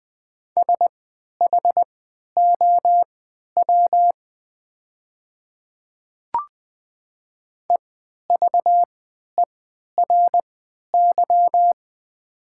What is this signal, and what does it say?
Morse code, heard as a keyed tone, about 20 wpm, spaced as for 11 wpm, 700 Hz. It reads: SHOW EVERY
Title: 3x Character Spacing - Top 300 Words - 20wpm